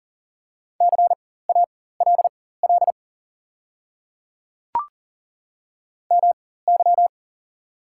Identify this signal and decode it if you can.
Morse code, heard as a keyed tone, about 37 wpm, spaced as for 18 wpm, 690 Hz. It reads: CALL MY
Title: 4x Character Spacing - Top 100 Words - 40wpm